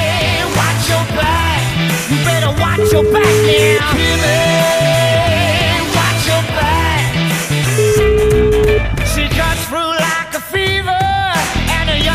Ringback tone (Europe, 425 Hz), with noise.